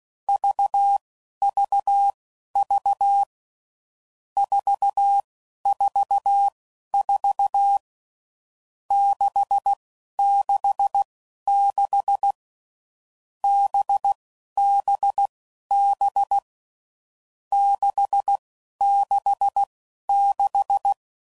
Morse code, heard as a keyed tone, about 16 wpm, spaced as for 11 wpm, 800 Hz. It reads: VVV 444 666 BBB 666
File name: Groups / Basic Letter Groups / VB46